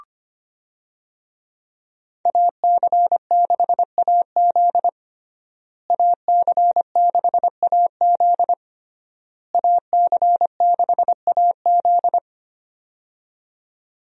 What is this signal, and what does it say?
Morse code, heard as a keyed tone, about 25 wpm, 700 Hz. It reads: AC6AZ AC6AZ AC6AZ
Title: USA Call Signs - 5 Characters - Repeated 3x - 25wpm